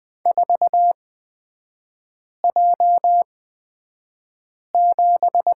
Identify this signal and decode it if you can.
Morse code, heard as a keyed tone, about 20 wpm, spaced as for 5 wpm, 700 Hz. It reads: 4J7
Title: 224 Lesson 36e - BT - Three Characters 20wpm